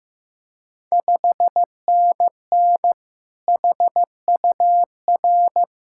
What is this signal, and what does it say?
Morse code, heard as a keyed tone, about 16 wpm, 700 Hz. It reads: 5NN HUR
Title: Ontario QSO Party Exchange 15wpm